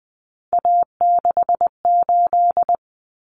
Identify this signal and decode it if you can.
Morse code, heard as a keyed tone, about 20 wpm, 700 Hz. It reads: A68
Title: Common Cut Numbers - 3-Digit 20wpm